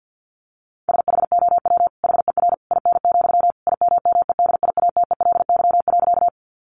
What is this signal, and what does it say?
Morse code, heard as a keyed tone, about 49 wpm, 700 Hz. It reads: SHOW HER IN, IMMEDIATELY.
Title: Sentences from Top 1100 Words 50wpm